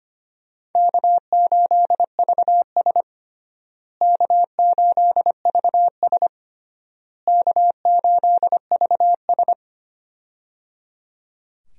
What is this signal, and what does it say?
Morse code, heard as a keyed tone, about 25 wpm, 700 Hz. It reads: K8VS K8VS K8VS
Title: USA Call Signs - 4 Characters - Repeated 3x - 25wpm